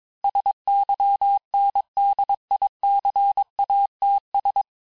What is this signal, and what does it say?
Morse code, heard as a keyed tone, about 22 wpm, 800 Hz. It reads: SYNDICATS
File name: texte clair vitesse 16 puis 21 mpm